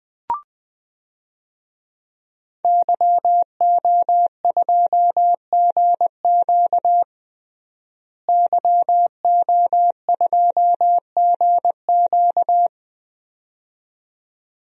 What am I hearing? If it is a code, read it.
Morse code, heard as a keyed tone, about 20 wpm, 700 Hz. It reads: YO2GQ YO2GQ
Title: DXCC-100 Call Signs - 5 Characters - Repeated 2x - 20wpm